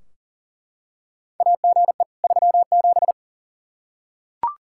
Morse code, heard as a keyed tone, about 40 wpm, 700 Hz. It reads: AGE 37
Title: Practice QSO - Exchange 3 - More Info - 40wpm